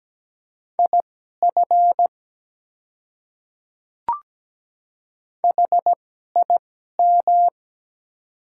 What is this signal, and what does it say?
Morse code, heard as a keyed tone, about 17 wpm, spaced as for 12 wpm, 700 Hz. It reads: IF HIM
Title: 2x Character Spacing - Top 100 Words - 17wpm